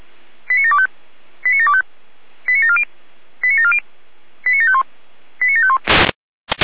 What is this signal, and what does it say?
Five-tone selective call (ZVEI), heard as quick runs of five tones, tones of about 70 ms. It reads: 89725 89725 89740 89740 89751 89751